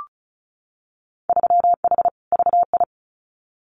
Morse code, heard as a keyed tone, about 34 wpm, 700 Hz. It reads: 3H VI
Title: Winter Field Day Exchanges Encore 35wpm